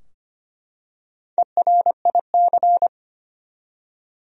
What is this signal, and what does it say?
Morse code, heard as a keyed tone, about 25 wpm, 700 Hz. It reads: ERIC
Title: USA Common Names - Max 5 Letters - 25wpm